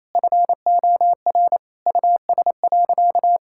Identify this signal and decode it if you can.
Morse code, heard as a keyed tone, about 28 wpm, 700 Hz. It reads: FOR US.
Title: Sentences from Top 1000 Words - Review Part 6 - 28wpm